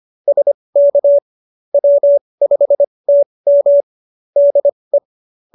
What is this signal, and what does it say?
Morse code, heard as a keyed tone, about 25 wpm, spaced as for 20 wpm, 570 Hz. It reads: SK W5TM DE
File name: QSO316 20